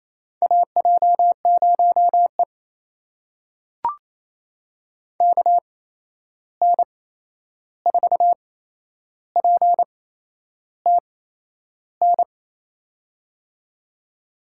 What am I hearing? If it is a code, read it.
Morse code, heard as a keyed tone, about 28 wpm, spaced as for 11 wpm, 700 Hz. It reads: AJ0E KN4PTN
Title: USA Call Signs - 8x Character Spacing - 28wpm